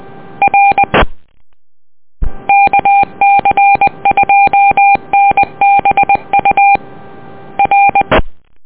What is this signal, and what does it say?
Morse code, heard as a keyed tone, about 20 wpm, 800 Hz. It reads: R KC2NBU R